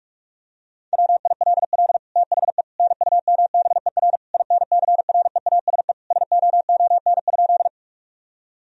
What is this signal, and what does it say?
Morse code, heard as a keyed tone, about 46 wpm, 700 Hz. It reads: WILL THE NUMBER INCREASE SOON?
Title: Sentences from Top 1000 Words - Review Part 4 - 45wpm